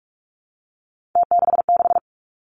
Morse code, heard as a keyed tone, about 46 wpm, 700 Hz. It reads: T66
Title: Three Challenging Letter-Number Combinations 45wpm